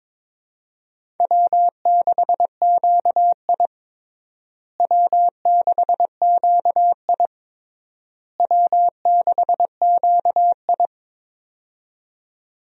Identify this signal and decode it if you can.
Morse code, heard as a keyed tone, about 22 wpm, 700 Hz. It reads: W6QI W6QI W6QI
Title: USA Call Signs - 4 Characters - Repeated 3x - 22wpm